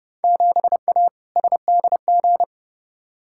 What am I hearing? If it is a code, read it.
Morse code, heard as a keyed tone, about 32 wpm, 700 Hz. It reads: 7A SDG
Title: Field Day Exchanges Encore - 30wpm